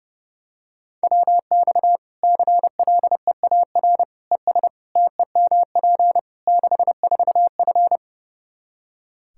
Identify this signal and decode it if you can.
Morse code, heard as a keyed tone, about 30 wpm, 700 Hz. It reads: WX CLEAR ES TEMP 64F
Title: K1USN SST QSO Practice 30wpm